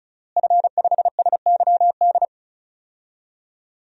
Morse code, heard as a keyed tone, about 36 wpm, 700 Hz. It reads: F5SYD
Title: DXCC-100 Call Signs - 5 Characters - Repeated 2x - 35wpm